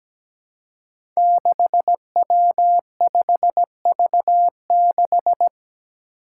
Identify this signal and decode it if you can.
Morse code, heard as a keyed tone, about 17 wpm, 700 Hz. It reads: 6W5V6
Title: Five Challenging Letter-Number Combinations 17wpm